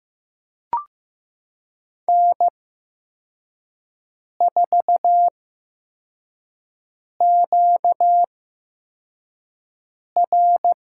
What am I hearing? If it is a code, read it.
Morse code, heard as a keyed tone, about 15 wpm, spaced as for 4 wpm, 700 Hz. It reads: N4QR
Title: USA Call Signs - 4 Characters - 8x Character Spacing - 15wpm